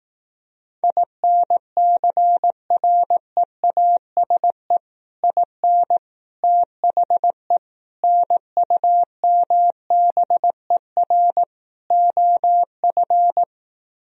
Morse code, heard as a keyed tone, about 18 wpm, 700 Hz. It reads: INCREASE IN THE NUMBER OF
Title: Speed-Racing - Sets of 5 Words 15wpm